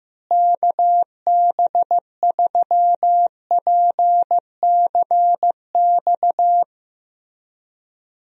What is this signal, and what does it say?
Morse code, heard as a keyed tone, about 15 wpm, 700 Hz. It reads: KB3PCX
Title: USA Call Signs - 6 Characters - Repeated 3x - 15wpm